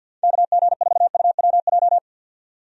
Morse code, heard as a keyed tone, about 51 wpm, 700 Hz. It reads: KG4UWJ
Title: USA Call Signs - 6 Characters - 50wpm